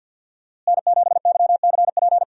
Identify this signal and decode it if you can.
Morse code, heard as a keyed tone, about 50 wpm, 700 Hz. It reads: N7YCP
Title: USA Call Signs - 5 Characters - Repeated 2x - 50wpm